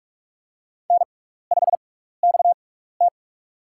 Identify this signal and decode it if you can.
Morse code, heard as a keyed tone, about 44 wpm, spaced as for 15 wpm, 700 Hz. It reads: N5XT
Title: USA Call Signs - 4 Characters - 6x Character Spacing - 45wpm